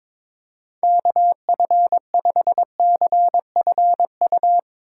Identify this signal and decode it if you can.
Morse code, heard as a keyed tone, about 22 wpm, 700 Hz. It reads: KF5CFU
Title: USA Call Signs - 6 Characters - 3x Character Spacing - 22wpm